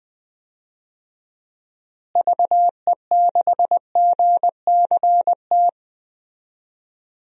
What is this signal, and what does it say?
Morse code, heard as a keyed tone, about 20 wpm, 700 Hz. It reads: VE6GCT
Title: DXCC-100 Call Signs - 6 Characters - Repeated 2x - 20wpm